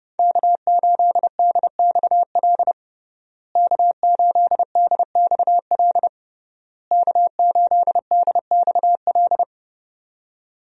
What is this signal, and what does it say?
Morse code, heard as a keyed tone, about 32 wpm, 700 Hz. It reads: K8DXL K8DXL K8DXL
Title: USA Call Signs - 5 Characters - Repeated 3x - 30wpm